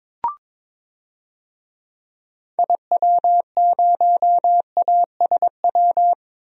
Morse code, heard as a keyed tone, about 22 wpm, 700 Hz. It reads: IW0ASW
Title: DXCC-100 Call Signs - 6 Characters - Repeated 2x - 22wpm